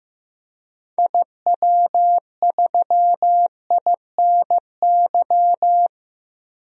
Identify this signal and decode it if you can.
Morse code, heard as a keyed tone, about 15 wpm, 700 Hz. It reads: IW3INY
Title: DXCC-100 Call Signs - 6 Characters - 15wpm